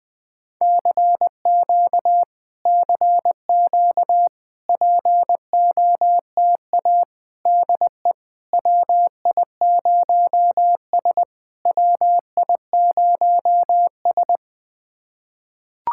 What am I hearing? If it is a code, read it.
Morse code, heard as a keyed tone, about 20 wpm, 700 Hz. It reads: CQ CQ POTA DE WI0S WI0S
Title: POTA QSO Practice - 20wpm